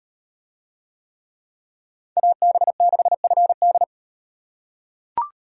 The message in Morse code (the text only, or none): AB6FD